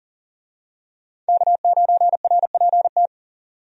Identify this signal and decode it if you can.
Morse code, heard as a keyed tone, about 40 wpm, 700 Hz. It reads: K9RPT
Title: USA Call Signs - 5 Characters - 40wpm